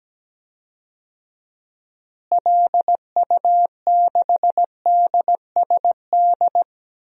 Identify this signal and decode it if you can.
Morse code, heard as a keyed tone, about 17 wpm, 700 Hz. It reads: LU6DSD
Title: DXCC-100 Call Signs - 6 Characters - Repeated 3x - 17wpm